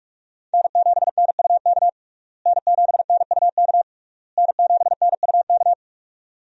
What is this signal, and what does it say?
Morse code, heard as a keyed tone, about 46 wpm, 700 Hz. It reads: N7NUK N7NUK N7NUK
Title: USA Call Signs - 5 Characters - Repeated 3x - 45wpm